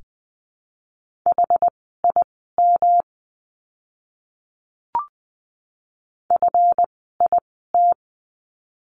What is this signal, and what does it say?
Morse code, heard as a keyed tone, about 20 wpm, spaced as for 14 wpm, 700 Hz. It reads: HIM FIT
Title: 139 Lesson 23f - Letter f - Words 20wpm